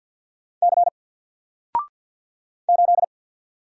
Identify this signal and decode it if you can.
Morse code, heard as a keyed tone, about 46 wpm, 700 Hz. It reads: C 8
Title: Single Character (Letter:Number:Punctuation:Prosign) 50wpm